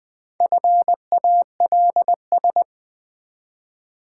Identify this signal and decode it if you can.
Morse code, heard as a keyed tone, about 20 wpm, 700 Hz. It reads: FALS
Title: Build A Word - 1 Letter At A Time - 5-Letter Words - 20wpm